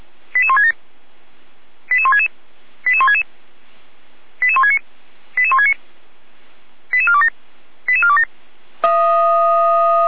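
Five-tone selective call (ZVEI), heard as quick runs of five tones, tones of about 70 ms. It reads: 88167 88166 88166 88169 88169 88427 88427
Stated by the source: ZVEI five-tone call received by SDR.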